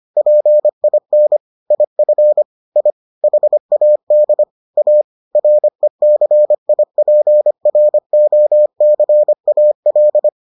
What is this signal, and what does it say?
Morse code, heard as a keyed tone, about 25 wpm, 600 Hz. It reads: PIN IF I HAD A RECIPROCAL